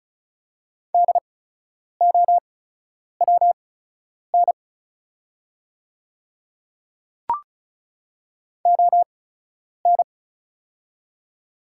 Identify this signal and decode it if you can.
Morse code, heard as a keyed tone, about 35 wpm, spaced as for 10 wpm, 700 Hz. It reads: DOWN ON
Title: 8x Character Spacing - Top 100 Words - 35wpm